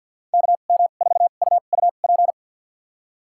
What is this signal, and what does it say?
Morse code, heard as a keyed tone, about 52 wpm, spaced as for 36 wpm, 700 Hz. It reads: KM4UUP
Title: USA Call Signs - 2x Character Spacing - 50wpm